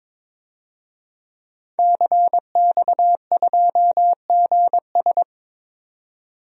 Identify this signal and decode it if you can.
Morse code, heard as a keyed tone, about 22 wpm, 700 Hz. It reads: CX2GS
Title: DXCC-100 Call Signs - 5 Characters - 22wpm